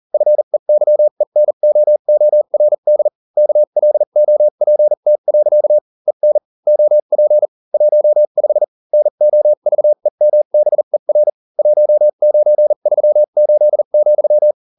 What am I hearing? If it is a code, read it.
Morse code, heard as a keyed tone, about 40 wpm, spaced as for 35 wpm, 600 Hz. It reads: FEYENOORD KLOPT. EN OP 15 NOVEMBER 1938,